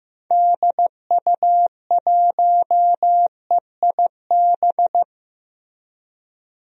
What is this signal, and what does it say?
Morse code, heard as a keyed tone, about 15 wpm, 700 Hz. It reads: DU1EIB
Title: DXCC-100 Call Signs - 6 Characters - Repeated 3x - 15wpm